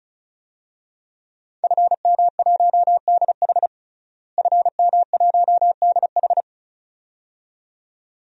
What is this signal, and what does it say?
Morse code, heard as a keyed tone, about 34 wpm, 700 Hz. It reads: FM1DH FM1DH
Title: DXCC-100 Call Signs - 5 Characters - Repeated 2x - 35wpm